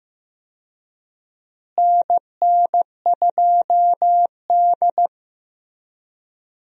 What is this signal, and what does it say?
Morse code, heard as a keyed tone, about 15 wpm, 700 Hz. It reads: NN2D
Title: USA Call Signs - 4 Characters - Repeated 3x - 15wpm